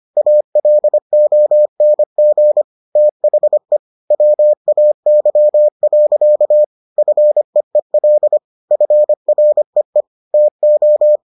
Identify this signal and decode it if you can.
Morse code, heard as a keyed tone, about 25 wpm, 600 Hz. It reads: ALONG THE WAY. FEEL FREE TO